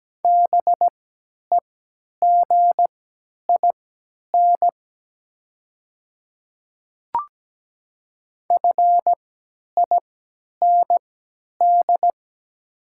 Morse code, heard as a keyed tone, about 17 wpm, spaced as for 10 wpm, 690 Hz. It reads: BEGIN FIND
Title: 3x Character Spacing - Top 300 Words - 17wpm